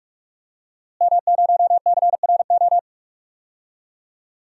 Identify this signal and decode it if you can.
Morse code, heard as a keyed tone, about 46 wpm, 700 Hz. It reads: M0CRO
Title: DXCC-100 Call Signs - 5 Characters - Repeated 2x - 45wpm